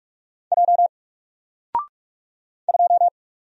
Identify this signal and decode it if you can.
Morse code, heard as a keyed tone, about 44 wpm, 700 Hz. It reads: J 2
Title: Single Letter-Number Alt 45wpm